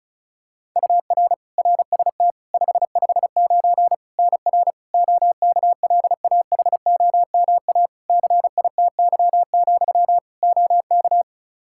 Morse code, heard as a keyed tone, about 34 wpm, 700 Hz. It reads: UR RST 559 NR OKLAHOMA CITY, OK
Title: Practice QSO - Exchange 1 - Beginning - 35wpm